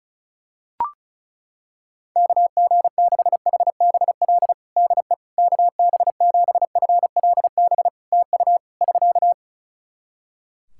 Morse code, heard as a keyed tone, about 36 wpm, 700 Hz. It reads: KG6HBL DE KB7FLB TU <SK>
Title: Practice QSO - Exchange 4 - Ending - 35wpm